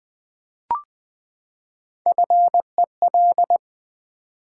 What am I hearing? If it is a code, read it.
Morse code, heard as a keyed tone, about 20 wpm, 700 Hz. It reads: FEL
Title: Build A Word - 3 Letters At A Time - 6-Letter Words - 20wpm